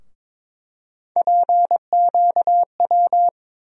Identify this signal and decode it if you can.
Morse code, heard as a keyed tone, about 22 wpm, 700 Hz. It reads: PQW
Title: USA Call Sign Postfixes - 3 Letters 22wpm